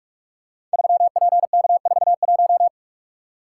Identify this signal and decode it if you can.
Morse code, heard as a keyed tone, about 44 wpm, 700 Hz. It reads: 3PK41
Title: Code Groups - 5 Letter:Numbers - 45wpm